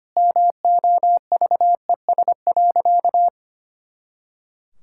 Morse code, heard as a keyed tone, about 25 wpm, 700 Hz. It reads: MOVES.
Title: Pangram Challenge v2 - 25wpm